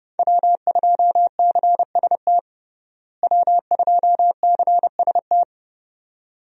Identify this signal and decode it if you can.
Morse code, heard as a keyed tone, about 31 wpm, 700 Hz. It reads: W2CST W2CST
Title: USA Call Signs - 5 Characters - Repeated 2x - 30wpm